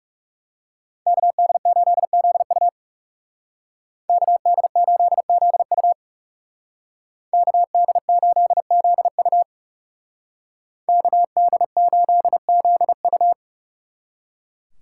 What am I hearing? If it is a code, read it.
Morse code, heard as a keyed tone, about 37 wpm, 700 Hz. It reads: KD8ZU KD8ZU KD8ZU KD8ZU
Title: Speed Racing - USA 2x2 Call Signs 30wpm